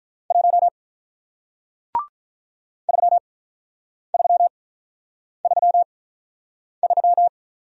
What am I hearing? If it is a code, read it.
Morse code, heard as a keyed tone, about 48 wpm, spaced as for 9 wpm, 700 Hz. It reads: 1 3333